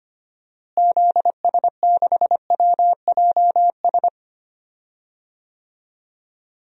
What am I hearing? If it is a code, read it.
Morse code, heard as a keyed tone, about 25 wpm, 700 Hz. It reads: ZS6WJS
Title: DXCC-100 Call Signs - 6 Characters - Repeated 3x - 25wpm